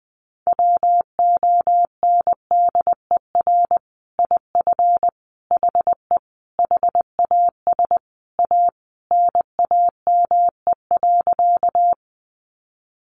Morse code, heard as a keyed tone, about 20 wpm, 700 Hz. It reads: WONDER IF HE HAS A NAME.
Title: Sentences from Top 500 Words 20wpm